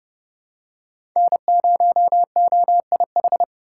Morse code, heard as a keyed tone, about 30 wpm, 700 Hz. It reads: N0OIH
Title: USA Call Signs - 5 Characters - 4x Character Spacing - 30wpm